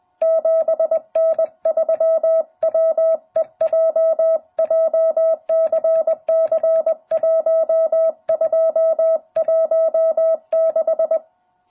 Morse code, heard as a keyed tone, about 20 wpm, 660 Hz. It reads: 7N3WEJ JCC1216